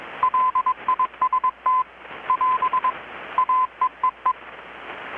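Morse code, heard as a keyed tone, about 21 wpm, 1030 Hz. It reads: LIST <AS> AEEE
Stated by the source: Morse from KPH's Night of Nights XX maritime radiotelegraphy event, received in Chiapas, Mexico about 3,800 km away on a Kenwood R-600 with a grounded 10 m long-wire antenna.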